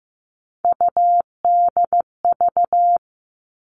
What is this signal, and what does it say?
Morse code, heard as a keyed tone, about 15 wpm, 700 Hz. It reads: UDV
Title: All Cut Numbers - 3-Digit 15wpm